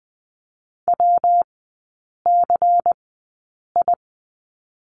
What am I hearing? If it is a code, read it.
Morse code, heard as a keyed tone, about 20 wpm, spaced as for 8 wpm, 700 Hz. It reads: WCI
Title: 127 Lesson 21e - Number 6 - Three Characters 20wpm